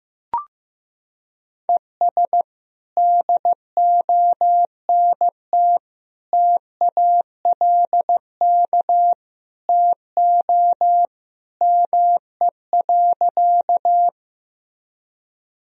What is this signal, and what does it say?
Morse code, heard as a keyed tone, about 16 wpm, 700 Hz. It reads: ES DONT TALK TO ME.
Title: Sentences from Top 400 Words and Abbreviations 15wpm